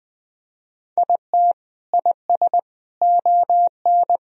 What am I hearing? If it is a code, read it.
Morse code, heard as a keyed tone, about 20 wpm, 700 Hz. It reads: IT IS ON
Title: Sentences from Top 1000 Words - Review Part 1 - 20wpm